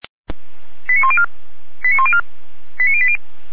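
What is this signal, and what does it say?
Five-tone selective call (ZVEI), heard as quick runs of five tones, tones of about 70 ms. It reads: 89104 89104 89080
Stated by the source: ZVEI five-tone call received on SDR, from the audio archive of feuerwehr-orlamuende.org (fire brigade Orlamünde).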